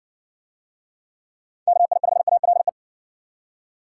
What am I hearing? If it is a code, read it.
Morse code, heard as a keyed tone, about 60 wpm, 700 Hz. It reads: KI5ALE